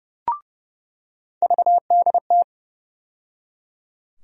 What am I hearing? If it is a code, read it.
Morse code, heard as a keyed tone, about 30 wpm, 700 Hz. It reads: VDT
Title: USA Call Sign Postfixes - 3 Letters 30wpm